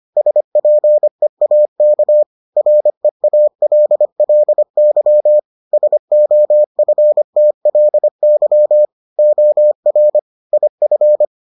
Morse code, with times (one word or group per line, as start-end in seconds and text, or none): 0.2-2.2 SPEAK
2.6-5.4 REALLY
5.7-8.9 SOFTLY
9.2-10.2 OR
10.5-11.2 IF